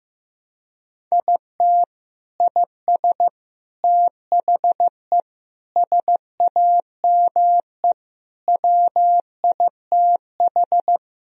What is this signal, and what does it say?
Morse code, heard as a keyed tone, about 15 wpm, 700 Hz. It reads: IT IS THE SAME WITH
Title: Sentences from Top 1000 Words - Review Part 3 - 15wpm